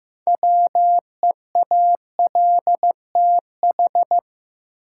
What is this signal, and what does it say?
Morse code, heard as a keyed tone, about 15 wpm, 700 Hz. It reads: WEALTH
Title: Six Letter Words - 15wpm